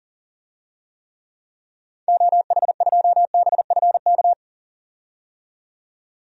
Morse code, heard as a keyed tone, about 40 wpm, 700 Hz. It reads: OH2BFK
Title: DXCC-100 Call Signs - 6 Characters - 40wpm